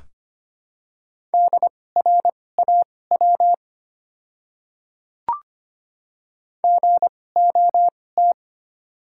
Morse code, heard as a keyed tone, about 25 wpm, spaced as for 18 wpm, 700 Hz. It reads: DRAW GOT
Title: 2x Character Spacing - Top 300 Words - 25wpm